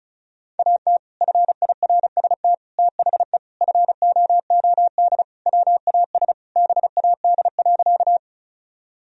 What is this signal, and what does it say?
Morse code, heard as a keyed tone, about 36 wpm, 700 Hz. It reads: AT FIRST THE FOOD WAS BAD.
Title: Sentences from Top 1000 Words - Review Part 1 - 35wpm